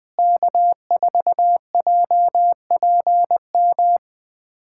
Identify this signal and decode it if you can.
Morse code, heard as a keyed tone, about 20 wpm, 700 Hz. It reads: K4JPM
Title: USA Callsign DE Callsign - 5 Characters - 20wpm